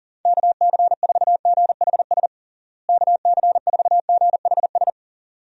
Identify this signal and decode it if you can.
Morse code, heard as a keyed tone, about 40 wpm, 700 Hz. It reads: KC4GHS KC4GHS
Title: USA Call Signs - 6 Characters - Repeated 2x - 40wpm